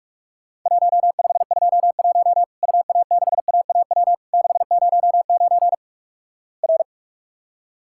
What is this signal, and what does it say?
Morse code, heard as a keyed tone, about 48 wpm, 700 Hz. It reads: 1521 UA6AAW 609 R
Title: Worked All Europe DX Contest - QTC Exchange - 45wpm